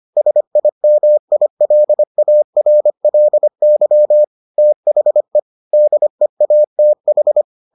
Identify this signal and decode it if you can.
Morse code, heard as a keyed tone, about 25 wpm, 600 Hz. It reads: SIMILARLY THE DEATH